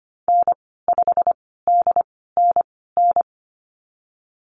Morse code, heard as a keyed tone, about 25 wpm, spaced as for 16 wpm, 700 Hz. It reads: N5DNN
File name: Lesson 15g - Number 5 - Call Signs 25wpm